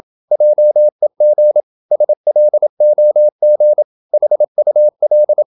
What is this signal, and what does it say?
Morse code, heard as a keyed tone, about 27 wpm, 600 Hz. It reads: JEG SLOG HUL
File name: Ferie QRQ 27wpm